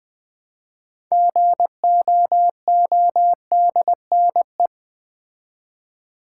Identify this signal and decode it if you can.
Morse code, heard as a keyed tone, about 20 wpm, 700 Hz. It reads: GOODNE
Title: Build A Word - 3 Letters At A Time - 8-Letter Words - 20wpm